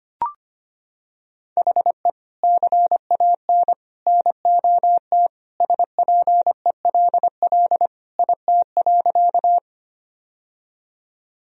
Morse code, heard as a keyed tone, about 25 wpm, 700 Hz. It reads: HE CAN NOT SPELL IT.
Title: Sentences from Top 1000 Words - Review Part 1 - 25wpm